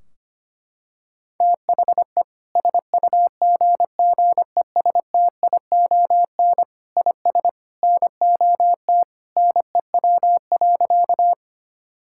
Morse code, heard as a keyed tone, about 25 wpm, 700 Hz. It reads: THE SUGGESTION IS NOT NEW.
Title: Sentences from Top 1600 Words 25wpm